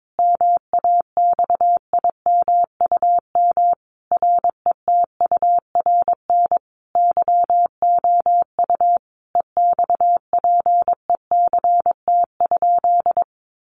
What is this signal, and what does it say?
Morse code, heard as a keyed tone, about 22 wpm, 700 Hz. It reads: MAXIMUM RETURN YOU EXPECT?
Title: Sentences from Top 1300 Words 22wpm